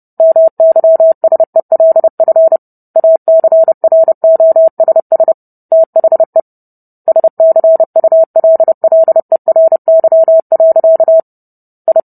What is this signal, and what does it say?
Morse code, heard as a keyed tone, about 30 wpm, 650 Hz. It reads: MYSELF ACROSS THE SCULLERY. I